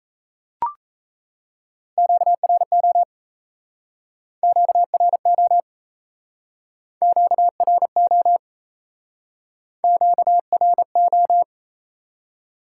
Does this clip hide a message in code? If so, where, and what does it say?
Yes: Morse code from 2.0 to 11.4 s, heard as a keyed tone, about 37 wpm, 700 Hz. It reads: QRO QRO QRO QRO